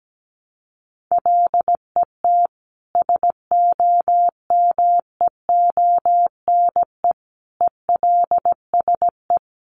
Morse code, heard as a keyed tone, about 17 wpm, 700 Hz. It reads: LET SOMEONE ELSE
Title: Sentences from Top 1200 Words 17wpm